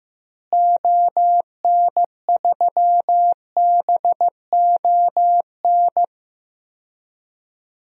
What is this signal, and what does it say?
Morse code, heard as a keyed tone, about 15 wpm, 700 Hz. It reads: ON3BON
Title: DXCC-100 Call Signs - 6 Characters - Repeated 2x - 15wpm